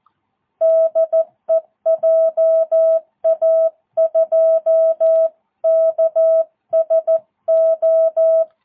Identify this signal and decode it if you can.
Morse code, heard as a keyed tone, about 14 wpm, 650 Hz. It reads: DE JA2KSO
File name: jh8cbh / 2022cw